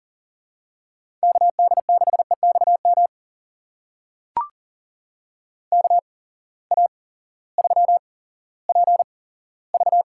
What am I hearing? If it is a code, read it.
Morse code, heard as a keyed tone, about 41 wpm, spaced as for 18 wpm, 700 Hz. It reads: KD6EXM KA3PV
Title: USA Call Signs - 6 Characters - 8x Character Spacing - 40wpm